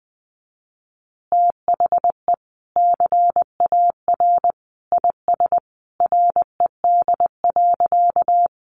Morse code, heard as a keyed tone, about 20 wpm, 700 Hz. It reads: THE CAR IS RED.
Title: Easy Phrases 20wpm